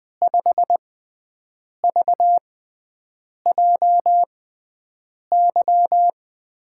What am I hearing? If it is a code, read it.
Morse code, heard as a keyed tone, about 20 wpm, spaced as for 7 wpm, 700 Hz. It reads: 5VJY
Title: USA Call Signs - 5 Characters - 6x Character Spacing - 20wpm